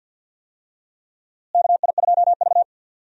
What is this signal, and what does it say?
Morse code, heard as a keyed tone, about 52 wpm, 700 Hz. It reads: KI2V
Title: USA Call Signs - 4 Characters - Repeated 3x - 50wpm